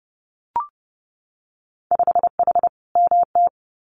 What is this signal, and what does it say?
Morse code, heard as a keyed tone, about 30 wpm, 700 Hz. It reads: 5H MT
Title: Winter Field Day Exchanges Encore 30wpm